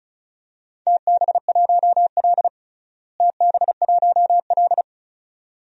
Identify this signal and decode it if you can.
Morse code, heard as a keyed tone, about 34 wpm, 700 Hz. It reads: TB1L TB1L
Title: DXCC-100 Call Signs - 4 Characters - Repeated 2x - 35wpm